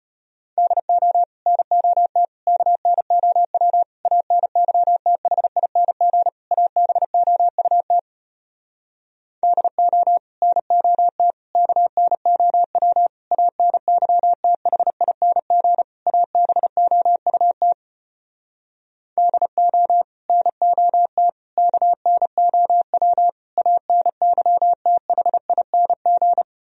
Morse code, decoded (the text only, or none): DO NOT KNOW ANYTHING ABOUT DO NOT KNOW ANYTHING ABOUT DO NOT KNOW ANYTHING